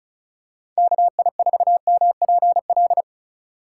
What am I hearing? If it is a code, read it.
Morse code, heard as a keyed tone, about 36 wpm, 700 Hz. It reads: KI4MPL
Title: USA Call Signs - 6 Characters - 2x Character Spacing - 35wpm